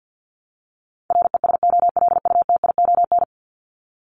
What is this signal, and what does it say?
Morse code, heard as a keyed tone, about 50 wpm, 700 Hz. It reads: RESOLUTION
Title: Top 1501-1750 Words 50wpm